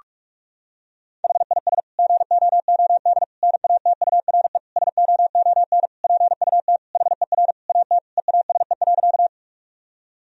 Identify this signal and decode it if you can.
Morse code, heard as a keyed tone, about 42 wpm, 700 Hz. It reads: HIS GOOD NATURE SOON PUT HER AT EASE.
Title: Sentences from Top 1000 Words - Review Part 1 - 45wpm